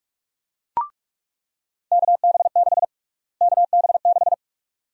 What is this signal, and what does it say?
Morse code, heard as a keyed tone, about 46 wpm, 700 Hz. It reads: KB6 KB6
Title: USA Call Sign Prefixes - Repeated 2x - 45wpm